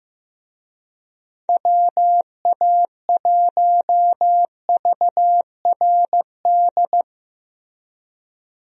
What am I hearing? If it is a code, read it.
Morse code, heard as a keyed tone, about 15 wpm, 700 Hz. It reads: WA1VRD
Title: USA Call Signs - 6 Characters - Repeated 3x - 15wpm